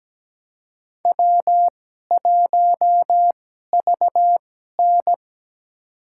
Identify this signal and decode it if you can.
Morse code, heard as a keyed tone, about 17 wpm, spaced as for 12 wpm, 700 Hz. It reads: W1VN
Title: USA Call Signs - 4 Characters - 2x Character Spacing - 17wpm